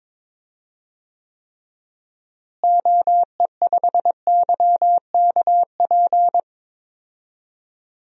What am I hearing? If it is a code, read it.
Morse code, heard as a keyed tone, about 22 wpm, 700 Hz. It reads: OE5YKP
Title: DXCC-100 Call Signs - 6 Characters - Repeated 2x - 22wpm